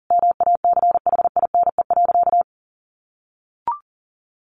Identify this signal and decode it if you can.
Morse code, heard as a keyed tone, about 40 wpm, 700 Hz. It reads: MACHINE.
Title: A Sound Of Thunder 40wpm